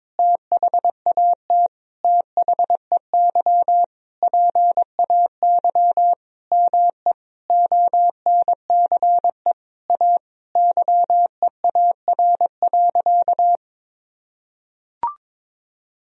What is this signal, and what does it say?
Morse code, heard as a keyed tone, about 22 wpm, 700 Hz. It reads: THAT THEY PAY ME ONCE A YEAR.
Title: Sentences from Top 1000 Words - Review Part 1 - 22wpm